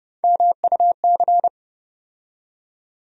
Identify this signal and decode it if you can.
Morse code, heard as a keyed tone, about 30 wpm, 700 Hz. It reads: MUC
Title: USA Call Sign Postfixes - 3 Letters 30wpm